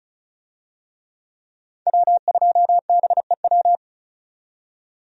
Morse code, heard as a keyed tone, about 36 wpm, 700 Hz. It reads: W2BEW
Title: USA Call Signs - 5 Characters - 35wpm